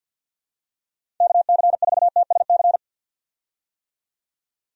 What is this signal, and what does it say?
Morse code, heard as a keyed tone, about 50 wpm, 700 Hz. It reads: KC4TSC
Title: USA Call Signs - 6 Characters - 50wpm